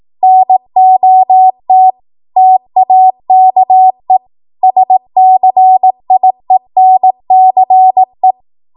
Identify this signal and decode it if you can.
Morse code, heard as a keyed tone, about 18 wpm, 750 Hz. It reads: NOT TAKE SCIENCE
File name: Morse / 18WPM